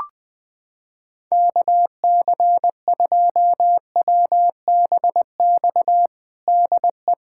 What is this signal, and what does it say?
Morse code, heard as a keyed tone, about 20 wpm, 700 Hz. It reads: KC2WBX DE
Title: USA Callsign DE Callsign - 6 Characters - 20wpm